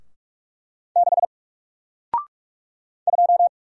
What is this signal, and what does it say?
Morse code, heard as a keyed tone, about 46 wpm, 700 Hz. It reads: 6 2
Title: Single Letter-Number Alt 45wpm